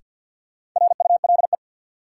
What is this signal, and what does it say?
Morse code, heard as a keyed tone, about 52 wpm, 700 Hz. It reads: RULE
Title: Sentences from Top 400 Words - New Word Drill - 50wpm